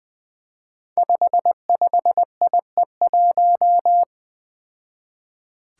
Morse code, heard as a keyed tone, about 20 wpm, 700 Hz. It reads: 55IE1
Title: Five Challenging Letter-Number Combinations 20wpm